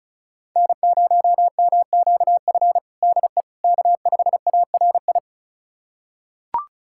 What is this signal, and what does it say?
Morse code, heard as a keyed tone, about 34 wpm, 700 Hz. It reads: N0MQF DE K5ARI
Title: USA Callsign DE Callsign 35wpm